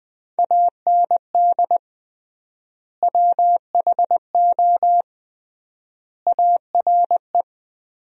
Morse code, heard as a keyed tone, about 20 wpm, 700 Hz. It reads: AND WHO ARE
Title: 3x Word-Spacing - Sentences from Top 100 Words - 20wpm